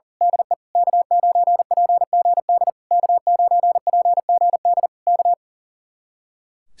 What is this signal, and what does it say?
Morse code, heard as a keyed tone, about 38 wpm, 700 Hz. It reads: DE K9PGD K9PGD K
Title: Speed-Racing Common QSO Phrases - 40wpm